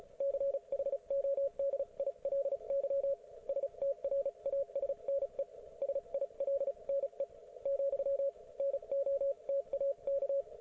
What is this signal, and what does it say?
Morse code, heard as a keyed tone, about 35 wpm, spaced as for 30 wpm, 550 Hz, with radio noise. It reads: CHODILY STRASNE SILNE , NO TAK